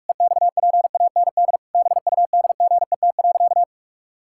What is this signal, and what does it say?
Morse code, heard as a keyed tone, about 44 wpm, 700 Hz. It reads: EXPAND BUDGET.
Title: Pangram Challenge v2 - 45wpm